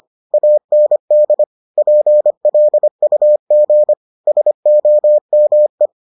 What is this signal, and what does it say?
Morse code, heard as a keyed tone, about 25 wpm, 600 Hz. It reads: AND PLUG SOME